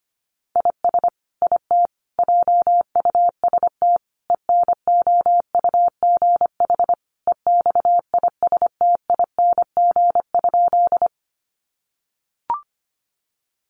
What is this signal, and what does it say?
Morse code, heard as a keyed tone, about 25 wpm, 700 Hz. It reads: IS IT JUST ENOUGH EXISTING?
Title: Sentences from Top 1500 Words 25wpm